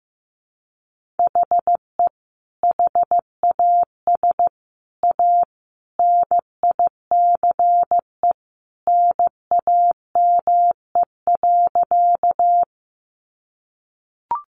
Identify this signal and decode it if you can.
Morse code, heard as a keyed tone, about 15 wpm, 700 Hz. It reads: HE HAS A NICE NAME.
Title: Sentences from Top 1100 Words 15wpm